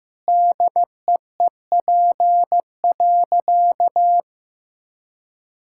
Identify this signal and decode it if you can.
Morse code, heard as a keyed tone, about 16 wpm, 700 Hz. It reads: DEEP.
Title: Sentences from Top 1000 Words - Review Part 6 - 15wpm